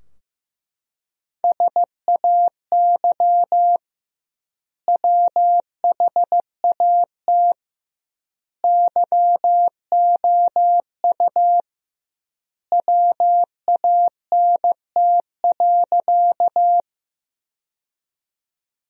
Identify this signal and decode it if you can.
Morse code, heard as a keyed tone, about 15 wpm, 700 Hz. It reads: SAY WHAT YOU WANT.
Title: 2x Word-Spacing - Sentences from Top 200 Words - 15wpm